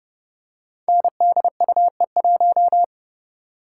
Morse code, heard as a keyed tone, about 29 wpm, 700 Hz. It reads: NDUE1
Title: Code Groups - 5 Letter:Numbers - 30wpm